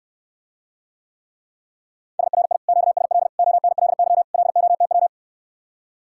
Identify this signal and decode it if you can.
Morse code, heard as a keyed tone, about 69 wpm, 700 Hz. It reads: SRI QSL ONLY LOTW